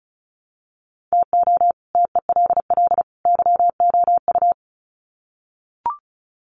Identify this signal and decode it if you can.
Morse code, heard as a keyed tone, about 34 wpm, 700 Hz. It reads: TO TELL YOU
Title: Sets of 3 Words 35wpm